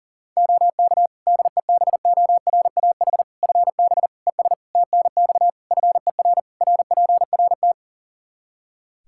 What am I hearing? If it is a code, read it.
Morse code, heard as a keyed tone, about 40 wpm, 700 Hz. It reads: OK DEBORAH FB ES TNX FER RPRT
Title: Practice QSO - Exchange 2 - Info - 40wpm